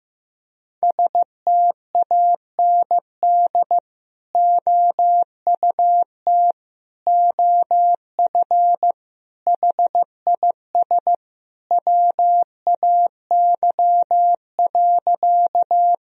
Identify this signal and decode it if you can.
Morse code, heard as a keyed tone, about 15 wpm, 700 Hz. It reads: STAND OUT OF HIS WAY.
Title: Sentences from Top 1000 Words - Review Part 1 - 15wpm